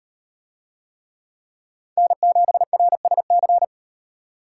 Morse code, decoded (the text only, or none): N7RSC